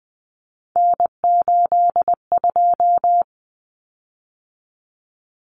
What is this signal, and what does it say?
Morse code, heard as a keyed tone, about 20 wpm, 700 Hz. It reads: N82
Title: Common Cut Numbers - 3-Digit 20wpm